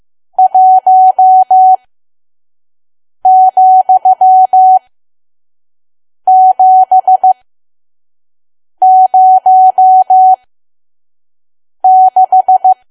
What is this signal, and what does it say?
Morse code, heard as a keyed tone, about 15 wpm, spaced as for 5 wpm, 750 Hz. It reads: 1,706
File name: Archive / 5 WPM files / PM